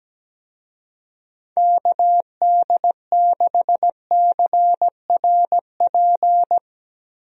Morse code, heard as a keyed tone, about 17 wpm, 700 Hz. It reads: KD6CRP
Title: USA Call Signs - 6 Characters - Repeated 3x - 17wpm